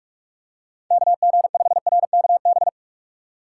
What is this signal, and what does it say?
Morse code, heard as a keyed tone, about 46 wpm, 700 Hz. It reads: KG5RKB
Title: USA Call Signs - 6 Characters - 3x Character Spacing - 45wpm